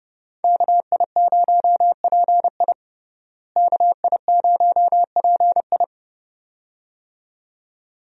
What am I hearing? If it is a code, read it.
Morse code, heard as a keyed tone, about 32 wpm, 700 Hz. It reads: KI0PI KI0PI
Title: USA Call Signs - 5 Characters - Repeated 2x - 30wpm